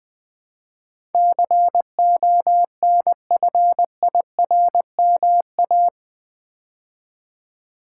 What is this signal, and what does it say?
Morse code, heard as a keyed tone, about 20 wpm, 700 Hz. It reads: CONFIRMA
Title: Build A Word - 4 Letters At A Time - 12-Letter Words - 20wpm